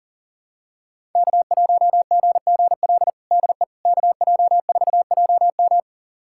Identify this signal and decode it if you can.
Morse code, heard as a keyed tone, about 40 wpm, 700 Hz. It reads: K1GGL DE KJ4JM
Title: USA Callsign DE Callsign - 5 Characters - 40wpm